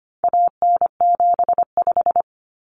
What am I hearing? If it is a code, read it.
Morse code, heard as a keyed tone, about 25 wpm, 700 Hz. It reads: AN75
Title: Common Cut Numbers - 4-Digit 25wpm